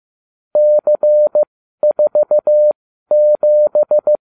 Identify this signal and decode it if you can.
Morse code, heard as a keyed tone, about 15 wpm, spaced as for 12 wpm, 600 Hz. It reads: C47